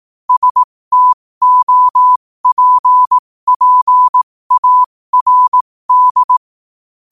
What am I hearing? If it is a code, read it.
Morse code, heard as a keyed tone, about 18 wpm, spaced as for 15 wpm, 1000 Hz. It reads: STOPPARD
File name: qotdfiles / 15wpm